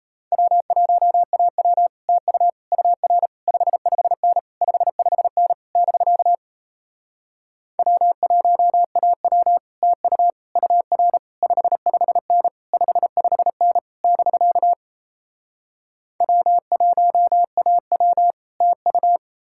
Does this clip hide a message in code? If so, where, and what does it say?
Yes: Morse code from 0.3 to 19.2 s, heard as a keyed tone, about 38 wpm, 700 Hz. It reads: W1AW TU UR 55N 55N <BK> W1AW TU UR 55N 55N <BK> W1AW TU